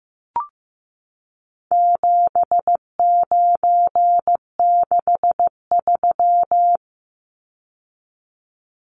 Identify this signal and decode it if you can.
Morse code, heard as a keyed tone, about 15 wpm, 700 Hz. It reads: 7963
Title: Numbers - 4-Digit 15wpm